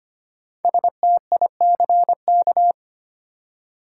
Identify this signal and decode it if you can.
Morse code, heard as a keyed tone, about 25 wpm, 700 Hz. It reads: STICK
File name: 5 Letter Words - First 3 Letters Given - 25wpm